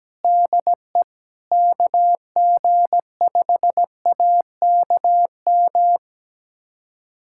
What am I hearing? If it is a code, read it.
Morse code, heard as a keyed tone, about 17 wpm, 700 Hz. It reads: DE KG5AKM
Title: USA Callsign DE Callsign - 6 Characters - 17wpm